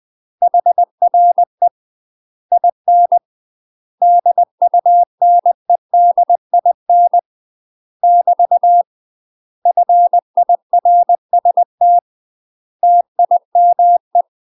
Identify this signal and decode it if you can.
Morse code, heard as a keyed tone, about 20 wpm, 700 Hz. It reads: HRE IN DUNEDIN <BT> FIRST TIME
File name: UK MASTER QSOs TRAINING / 20WPM / UK MASTER QSOs 0031